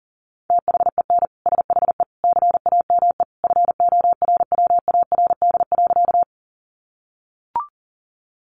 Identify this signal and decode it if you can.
Morse code, heard as a keyed tone, about 37 wpm, 700 Hz. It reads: THEN SHE CAME FORWARD.
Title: Sentences from Top 1300 Words 40wpm